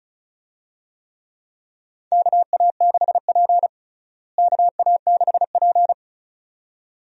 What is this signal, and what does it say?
Morse code, heard as a keyed tone, about 36 wpm, 700 Hz. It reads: KA6P KA6P
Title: USA Call Signs - 4 Characters - Repeated 2x - 35wpm